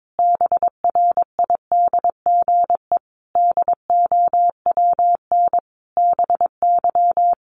Morse code, heard as a keyed tone, about 22 wpm, 700 Hz. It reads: BRIDGE DOWN BY
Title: Sentences from Top 1200 Words 22wpm